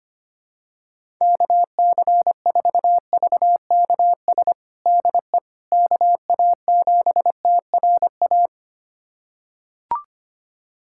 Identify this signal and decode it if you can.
Morse code, heard as a keyed tone, about 25 wpm, 700 Hz. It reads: KC4VKS DE KA7TRA
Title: USA Callsign DE Callsign 25wpm